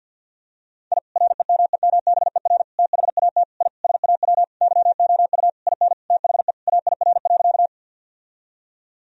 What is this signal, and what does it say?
Morse code, heard as a keyed tone, about 47 wpm, 700 Hz. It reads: I REMEMBER THAT I SAW YOU IN THE AIR.
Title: Sentences from Top 1000 Words - Review Part 1 - 50wpm